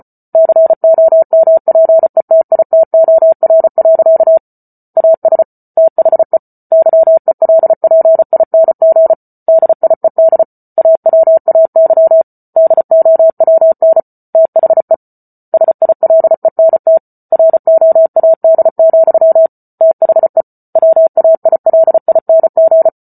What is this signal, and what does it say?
Morse code, heard as a keyed tone, about 34 wpm, 650 Hz. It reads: COMPETITOR. AS THE YELPING DIED AWAY DOWN THE SILENT ROAD, THE WAILING